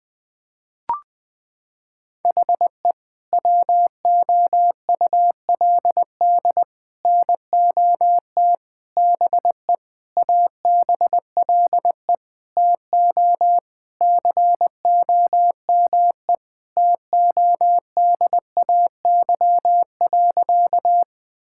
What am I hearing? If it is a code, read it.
Morse code, heard as a keyed tone, about 20 wpm, 700 Hz. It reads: HE WOULD NOT BE ABLE TO COME TODAY.
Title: Sentences with 5-Word Phrases - 20wpm